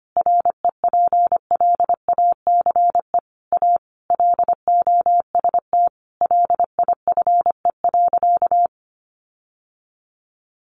Morse code, heard as a keyed tone, about 25 wpm, 700 Hz. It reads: REPLACE A LOST LIFE.
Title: Sentences from Top 1400 Words 25wpm